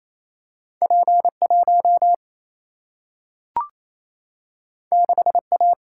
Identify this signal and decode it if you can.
Morse code, heard as a keyed tone, about 28 wpm, 700 Hz. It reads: P 1 6 A
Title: Single-Letter Single-Number 28wpm